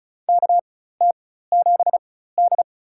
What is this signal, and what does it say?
Morse code, heard as a keyed tone, about 36 wpm, spaced as for 16 wpm, 700 Hz. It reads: KT7D
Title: USA Call Signs - 4 Characters - 4x Character Spacing - 35wpm